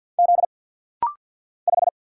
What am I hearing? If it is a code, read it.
Morse code, heard as a keyed tone, about 50 wpm, 700 Hz. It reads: Z 5
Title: Single Letter-Number - Rapid-Fire 50wpm